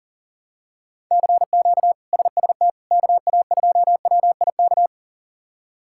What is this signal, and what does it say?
Morse code, heard as a keyed tone, about 40 wpm, 700 Hz. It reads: CQ SST KA2WIK
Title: K1USN SST QSO Practice - 40wpm